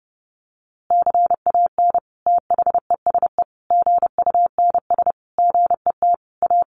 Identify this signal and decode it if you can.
Morse code, heard as a keyed tone, about 28 wpm, 700 Hz. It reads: CAN THESE GUNS GET A
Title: A Sound Of Thunder 30wpm